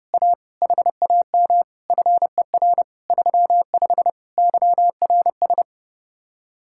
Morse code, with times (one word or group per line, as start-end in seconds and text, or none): 0.1-0.3 A
0.6-1.6 HAM
1.9-2.8 FER
3.1-4.1 35
4.4-5.6 YRS